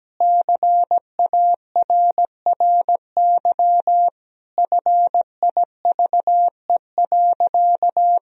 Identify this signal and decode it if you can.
Morse code, heard as a keyed tone, about 17 wpm, 700 Hz. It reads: CARRY FIVE.
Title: Sentences from Top 400 Words and Abbreviations 17wpm